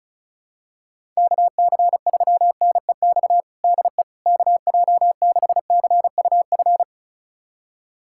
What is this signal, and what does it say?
Morse code, heard as a keyed tone, about 36 wpm, 700 Hz. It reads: KC3NEX DE KJ6CUF
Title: USA Callsign DE Callsign - 6 Characters - 35wpm